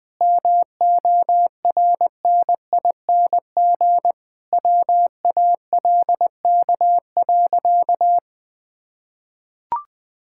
Morse code, heard as a keyed tone, about 20 wpm, 700 Hz. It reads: MORNING WALK.
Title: Sentences from Top 1000 Words - Review Part 5 - 20wpm